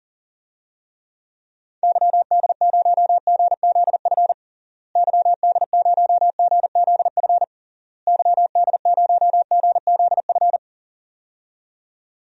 Morse code, heard as a keyed tone, about 40 wpm, 700 Hz. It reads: YD0GZF YD0GZF YD0GZF
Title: DXCC-100 Call Signs - 6 Characters - Repeated 3x - 40wpm